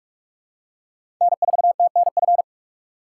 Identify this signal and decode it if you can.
Morse code, heard as a keyed tone, about 46 wpm, 700 Hz. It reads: N4TNF
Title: USA Call Signs - 5 Characters - 3x Character Spacing - 45wpm